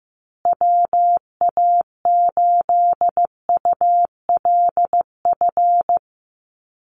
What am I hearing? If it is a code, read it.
Morse code, heard as a keyed tone, about 15 wpm, 700 Hz. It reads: WA8ULF
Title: USA Call Signs Repeated 2x 15wpm